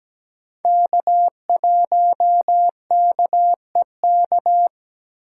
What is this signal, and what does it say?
Morse code, heard as a keyed tone, about 17 wpm, 700 Hz. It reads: K1KEK
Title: USA Call Signs - 5 Characters - 8x Character Spacing - 17wpm